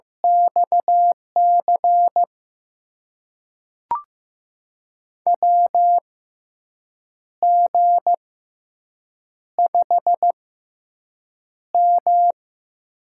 Morse code, heard as a keyed tone, about 15 wpm, spaced as for 6 wpm, 700 Hz. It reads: XC WG5M
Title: USA Call Signs - 6x Character Spacing - 15wpm